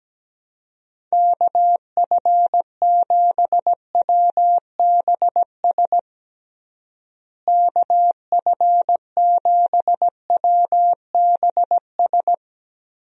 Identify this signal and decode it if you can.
Morse code, heard as a keyed tone, about 17 wpm, 700 Hz. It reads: KF7WBS KF7WBS
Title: USA Call Signs - 6 Characters - Repeated 3x - 17wpm